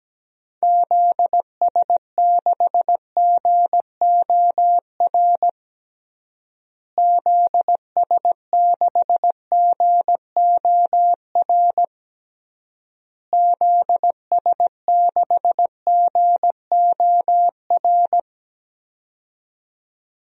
Morse code, heard as a keyed tone, about 17 wpm, 700 Hz. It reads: ZS6GOR ZS6GOR ZS6GOR
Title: DXCC-100 Call Signs - 6 Characters - Repeated 3x - 17wpm